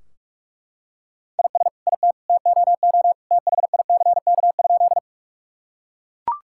Morse code, heard as a keyed tone, about 43 wpm, 700 Hz. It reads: IS IT TOO THICK?
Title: Sentences from Top 1600 Words 45wpm